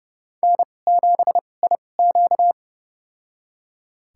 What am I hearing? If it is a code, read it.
Morse code, heard as a keyed tone, about 30 wpm, spaced as for 22 wpm, 700 Hz. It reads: N7IQ
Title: USA Call Signs - 4 Characters - 2x Character Spacing - 30wpm